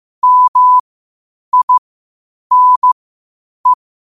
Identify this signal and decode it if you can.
Morse code, heard as a keyed tone, about 15 wpm, spaced as for 8 wpm, 1000 Hz. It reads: MINE
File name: qotdfiles / 8wpm